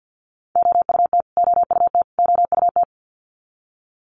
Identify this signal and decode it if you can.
Morse code, heard as a keyed tone, about 50 wpm, 700 Hz. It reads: OUT OUT OUT
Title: Top 300 Words Repeated 50wpm